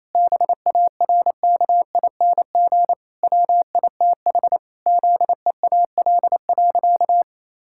Morse code, heard as a keyed tone, about 28 wpm, 700 Hz. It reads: BARKING WITH ZEAL.
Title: Pangram Challenge v2 - 28wpm